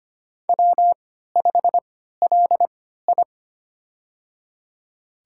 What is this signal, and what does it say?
Morse code, heard as a keyed tone, about 25 wpm, spaced as for 14 wpm, 700 Hz. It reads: W5LI
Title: USA Call Signs - 4 Characters - 3x Character Spacing - 25wpm